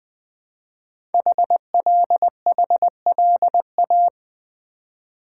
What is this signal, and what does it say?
Morse code, heard as a keyed tone, about 20 wpm, 700 Hz. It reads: HLHLA
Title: Code Groups - 5 Letters - 20wpm